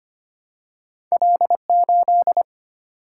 Morse code, heard as a keyed tone, about 25 wpm, 700 Hz. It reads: L8
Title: Single-Letter Single-Number 25wpm